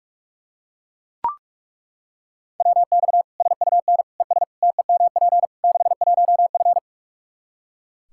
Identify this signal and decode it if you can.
Morse code, heard as a keyed tone, about 42 wpm, 700 Hz. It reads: WX SUN ES TEMP 61F
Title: Practice QSO - Exchange 2 - Info - 45wpm